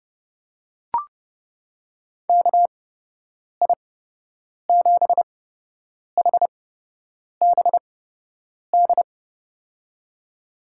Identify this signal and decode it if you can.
Morse code, heard as a keyed tone, about 29 wpm, spaced as for 8 wpm, 700 Hz. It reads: KI7HBD
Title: USA Call Signs - 8x Character Spacing - 30wpm